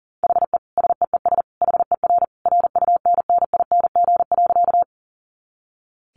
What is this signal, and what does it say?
Morse code, heard as a keyed tone, about 40 wpm, 700 Hz. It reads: HE SEES HER RUNNING.
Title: Additional Easy Phrases 40wpm